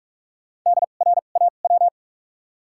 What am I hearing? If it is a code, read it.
Morse code, heard as a keyed tone, about 46 wpm, spaced as for 32 wpm, 700 Hz. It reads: DRAW
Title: 2x Character Spacing - Top 300 Words - 45wpm